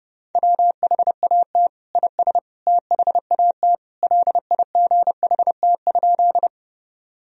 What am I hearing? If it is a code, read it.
Morse code, heard as a keyed tone, about 32 wpm, 700 Hz. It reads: WHAT IS THAT LIGHT?
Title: Sentences from Top 1000 Words - Review Part 1 - 30wpm